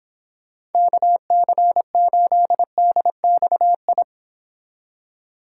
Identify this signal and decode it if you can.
Morse code, heard as a keyed tone, about 31 wpm, 700 Hz. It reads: KC8DXI